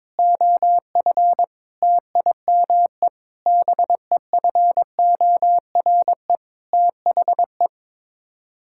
Morse code, heard as a keyed tone, about 22 wpm, 700 Hz. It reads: OF TIME BEFORE THE
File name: Sets of 9 Words - 22wpm